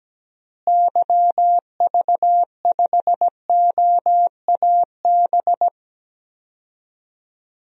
Morse code, heard as a keyed tone, about 17 wpm, 700 Hz. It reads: YV5OAB
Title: DXCC-100 Call Signs - 6 Characters - Repeated 3x - 17wpm